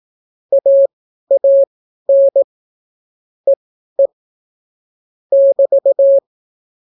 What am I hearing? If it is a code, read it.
Morse code, heard as a keyed tone, about 18 wpm, spaced as for 12 wpm, 550 Hz. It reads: AAN EE <BT>